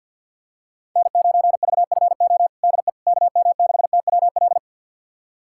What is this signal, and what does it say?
Morse code, heard as a keyed tone, about 52 wpm, 690 Hz. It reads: N9VFO DE KM6TWL
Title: USA Callsign DE Callsign 50wpm